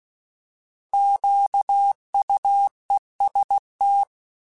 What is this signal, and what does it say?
Morse code, heard as a keyed tone, about 16 wpm, 800 Hz. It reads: QUEST